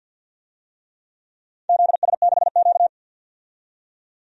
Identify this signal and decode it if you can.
Morse code, heard as a keyed tone, about 50 wpm, 700 Hz. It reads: ZS6Q
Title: DXCC-100 Call Signs - 4 Characters - 50wpm